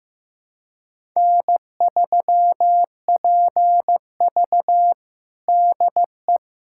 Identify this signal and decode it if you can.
Morse code, heard as a keyed tone, about 15 wpm, 700 Hz. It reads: N3PV DE
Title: USA Callsign DE Callsign - 4 Characters - 15wpm